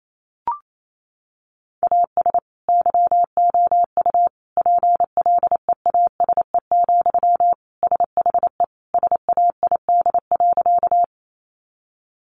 Morse code, heard as a keyed tone, about 28 wpm, 700 Hz. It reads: AS YOU PLEASE, SHE SAID.
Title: Sentences from Top 1500 Words 28wpm